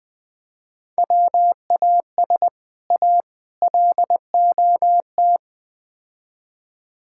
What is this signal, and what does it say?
Morse code, heard as a keyed tone, about 20 wpm, 700 Hz. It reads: WAS A LOT
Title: Build A Phrase - 4-Word Phrases - 20wpm